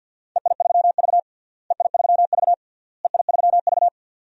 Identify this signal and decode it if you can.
Morse code, heard as a keyed tone, about 48 wpm, 700 Hz. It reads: EI3V EI3V EI3V
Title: DXCC-100 Call Signs - 4 Characters - Repeated 3x - 50wpm